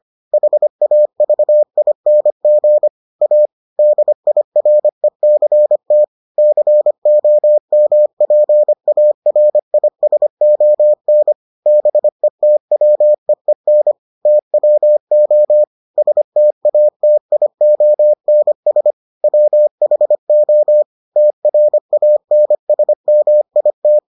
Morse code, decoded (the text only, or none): HAVING A DIRECT COMPARISON BETWEEN TWO STATIONS WHO TRANSMIT